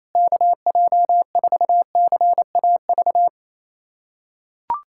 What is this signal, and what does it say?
Morse code, heard as a keyed tone, about 28 wpm, 700 Hz. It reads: KJ4CAV
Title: USA Callsign DE Callsign 28wpm